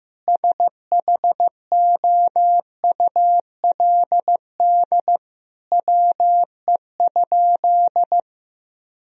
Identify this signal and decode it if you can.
Morse code, heard as a keyed tone, about 15 wpm, 700 Hz. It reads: SHOULD WE?
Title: Sentences from Top 1000 Words - Review Part 3 - 15wpm